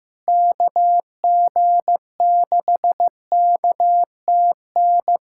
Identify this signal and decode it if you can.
Morse code, heard as a keyed tone, about 16 wpm, 700 Hz. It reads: KG6KTN
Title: USA Callsign DE Callsign 15wpm